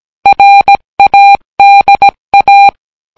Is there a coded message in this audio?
Yes: Morse code from 0.3 to 2.7 s, heard as a keyed tone, about 18 wpm, 800 Hz. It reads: RADA